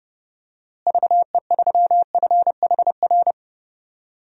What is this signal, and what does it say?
Morse code, heard as a keyed tone, about 32 wpm, 700 Hz. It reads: VE3FHR
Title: DXCC-100 Call Signs - 6 Characters - Repeated 2x - 30wpm